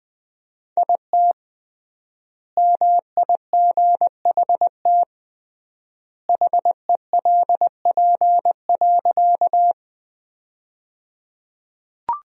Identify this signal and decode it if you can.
Morse code, heard as a keyed tone, about 20 wpm, 700 Hz. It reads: IT MIGHT HELP.
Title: 3x Word-Spacing - Sentences from Top 300 Words - 20wpm